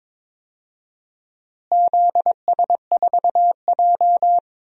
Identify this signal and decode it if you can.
Morse code, heard as a keyed tone, about 22 wpm, 700 Hz. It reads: ZS4J
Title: DXCC-100 Call Signs - 4 Characters - Repeated 3x - 22wpm